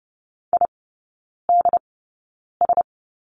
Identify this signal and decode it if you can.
Morse code, heard as a keyed tone, about 30 wpm, spaced as for 9 wpm, 700 Hz. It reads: IDS
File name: 081 Lesson 14e - Number 2 - Three Characters 30wpm